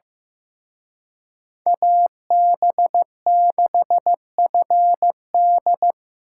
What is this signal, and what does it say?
Morse code, heard as a keyed tone, about 18 wpm, 700 Hz. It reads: AB6FD
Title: Speed Racing - USA Call Signs 15wpm